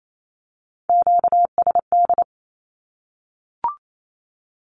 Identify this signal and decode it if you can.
Morse code, heard as a keyed tone, about 28 wpm, 700 Hz. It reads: QSD
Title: Q-Codes Repeated 3x 28wpm